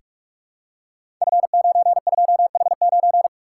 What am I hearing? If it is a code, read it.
Morse code, heard as a keyed tone, about 44 wpm, 700 Hz. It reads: F92H9
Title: Code Groups - 5 Letter:Numbers - 45wpm